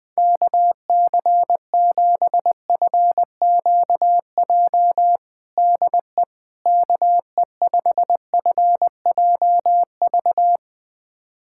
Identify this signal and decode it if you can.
Morse code, heard as a keyed tone, about 20 wpm, 700 Hz. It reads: KC7FQJ DE KE5FJV
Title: USA Callsign DE Callsign - 6 Characters - 20wpm